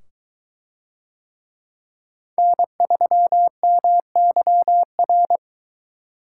Morse code, heard as a keyed tone, about 23 wpm, 700 Hz. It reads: N3MYR